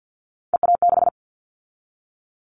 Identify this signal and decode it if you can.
Morse code, heard as a keyed tone, about 50 wpm, 700 Hz. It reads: EA6
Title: Callsign Prefixes for Top 50 DXCC - 50wpm